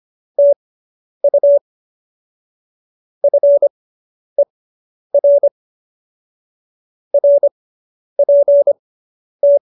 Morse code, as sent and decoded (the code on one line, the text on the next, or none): - ..- / ..-. .-. / .-. .--. -
TU FER RPT